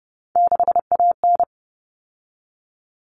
Morse code, heard as a keyed tone, about 30 wpm, 700 Hz. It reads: 6AN
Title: All Cut Numbers - 3-Digit 30wpm